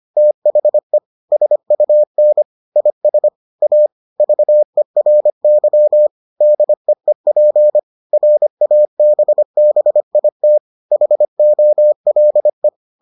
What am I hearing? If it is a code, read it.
Morse code, heard as a keyed tone, about 25 wpm, 600 Hz. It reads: THE SUN IS A VERY DEEP RABBIT HOLE